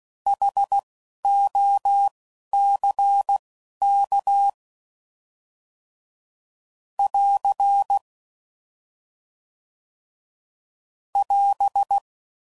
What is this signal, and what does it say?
Morse code, heard as a keyed tone, about 16 wpm, spaced as for 11 wpm, 800 Hz. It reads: HOCK <AR> <AS>